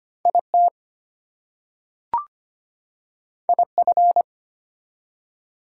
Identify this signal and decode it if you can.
Morse code, heard as a keyed tone, about 25 wpm, 700 Hz. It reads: IT IF
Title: Two Letter Words - 25wpm